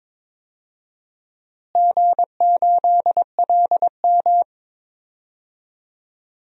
Morse code, heard as a keyed tone, about 22 wpm, 700 Hz. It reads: G8LM
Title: DXCC-100 Call Signs - 4 Characters - Repeated 3x - 22wpm